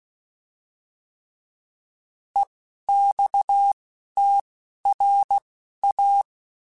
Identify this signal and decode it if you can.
Morse code, heard as a keyed tone, about 16 wpm, spaced as for 11 wpm, 800 Hz. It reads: EXTRA
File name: fromSMs / FromSMs mp3s / Classes@10